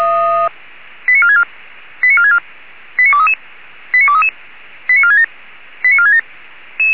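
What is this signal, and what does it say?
Five-tone selective call (ZVEI), heard as quick runs of five tones, tones of about 70 ms. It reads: 89464 89464 89233 89233 89467 89467
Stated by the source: ZVEI five-tone selective call received on SDR.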